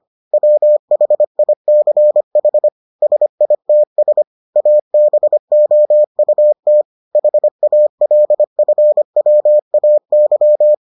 Morse code, heard as a keyed tone, about 25 wpm, 600 Hz. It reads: WHICH SITS ABOUT HALFWAY